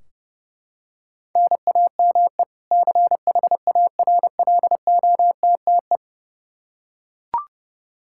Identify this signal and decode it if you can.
Morse code, heard as a keyed tone, about 32 wpm, 700 Hz. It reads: NAME CHARLOTTE
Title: Practice QSO - Exchange 1 - Beginning - 30wpm